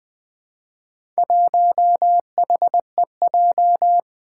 Morse code, heard as a keyed tone, about 20 wpm, 700 Hz. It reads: 1HEJ
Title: Four Challenging Letter-Number Combinations 20wpm